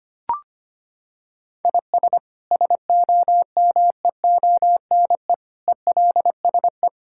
Morse code, heard as a keyed tone, about 25 wpm, 700 Hz. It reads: IS SOMEONE ELSE
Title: Sentences from Top 1000 Words - Review Part 5 - 25wpm